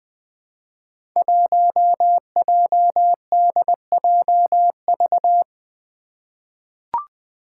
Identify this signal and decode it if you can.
Morse code, heard as a keyed tone, about 20 wpm, 700 Hz. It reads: 1JDJV
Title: Five Challenging Letter-Number Combinations 20wpm